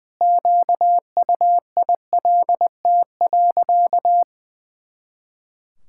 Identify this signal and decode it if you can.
Morse code, heard as a keyed tone, about 19 wpm, 700 Hz. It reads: QUILT.
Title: Pangram Challenge v2 - 20wpm